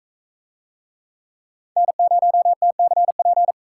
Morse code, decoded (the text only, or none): N0TCP